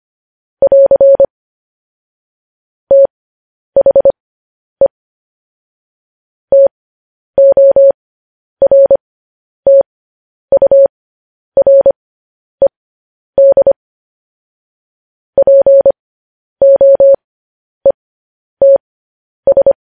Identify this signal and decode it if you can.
Morse code, heard as a keyed tone, about 25 wpm, spaced as for 10 wpm, 550 Hz. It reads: <AR> THE TORTURED POETS